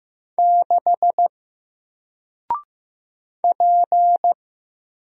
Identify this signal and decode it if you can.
Morse code, heard as a keyed tone, about 15 wpm, 700 Hz. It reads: 6 P
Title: Single Letter-Number 15wpm